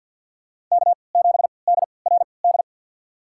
Morse code, heard as a keyed tone, about 46 wpm, spaced as for 27 wpm, 700 Hz. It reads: K7DRD
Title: USA Call Signs - 3x Character Spacing - 50wpm